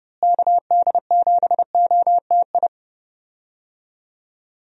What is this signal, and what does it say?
Morse code, heard as a keyed tone, about 31 wpm, 700 Hz. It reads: KD7OTI
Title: USA Call Signs - 6 Characters - Repeated 2x - 30wpm